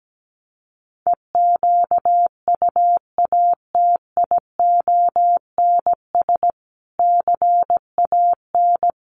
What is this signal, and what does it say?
Morse code, heard as a keyed tone, about 17 wpm, 700 Hz. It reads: EQUATIONS CAN
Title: Sets of 5 Words 17wpm